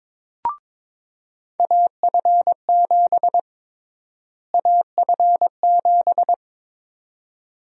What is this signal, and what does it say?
Morse code, heard as a keyed tone, about 22 wpm, 700 Hz. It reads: AF7 AF7
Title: USA Call Sign Prefixes - Repeated 2x - 22wpm